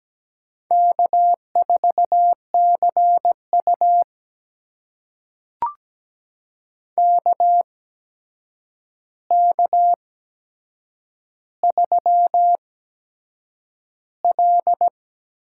Morse code, heard as a keyed tone, about 17 wpm, spaced as for 7 wpm, 700 Hz. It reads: K4CU KK3L
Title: USA Call Signs - 4 Characters - 8x Character Spacing - 17wpm